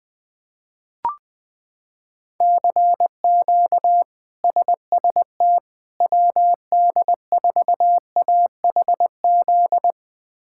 Morse code, heard as a keyed tone, about 20 wpm, 700 Hz. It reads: CQ SST WD4AHZ
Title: K1USN SST QSO Practice - 20wpm